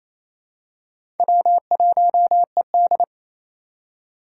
Morse code, heard as a keyed tone, about 28 wpm, 700 Hz. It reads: W1ED
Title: USA Call Signs - 4 Characters - 2x Character Spacing - 28wpm